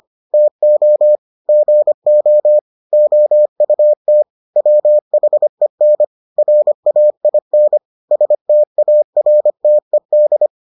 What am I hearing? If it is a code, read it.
Morse code, heard as a keyed tone, about 25 wpm, 600 Hz. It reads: TO GO OUT WHEN RAIN STARTED